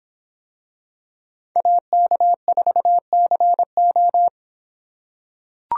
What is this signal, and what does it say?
Morse code, heard as a keyed tone, about 25 wpm, 700 Hz. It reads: AK4CO